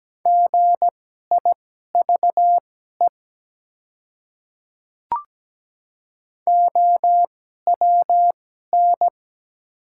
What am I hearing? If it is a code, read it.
Morse code, heard as a keyed tone, about 17 wpm, spaced as for 12 wpm, 700 Hz. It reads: GIVE OWN
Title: 2x Character Spacing - Top 300 Words - 17wpm